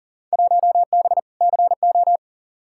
Morse code, heard as a keyed tone, about 38 wpm, 700 Hz. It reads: 1B CO
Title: Field Day Exchanges - 40wpm